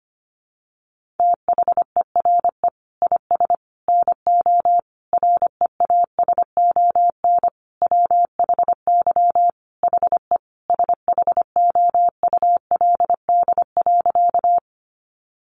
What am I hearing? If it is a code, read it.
Morse code, heard as a keyed tone, about 25 wpm, 700 Hz. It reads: THERE IS NO REASON WHY HE SHOULD.
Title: Sentences from Top 700 Words 25wpm